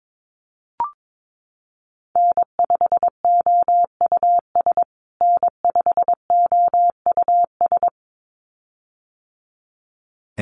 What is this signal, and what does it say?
Morse code, heard as a keyed tone, about 22 wpm, 700 Hz. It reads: N5OUS N5OUS
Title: USA Call Signs Repeated 2x 22wpm